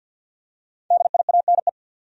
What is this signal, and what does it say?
Morse code, heard as a keyed tone, about 48 wpm, 700 Hz. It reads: DIANE
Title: USA Common Names - 50wpm